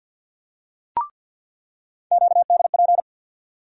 Morse code, heard as a keyed tone, about 47 wpm, 700 Hz. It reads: QDP
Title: USA Call Sign Postfixes - 3 Letters 50wpm